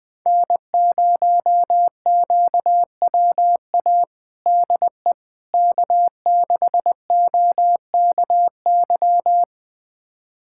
Morse code, heard as a keyed tone, about 20 wpm, 700 Hz. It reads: N0QWA DE K6OKY
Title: USA Callsign DE Callsign - 5 Characters - 20wpm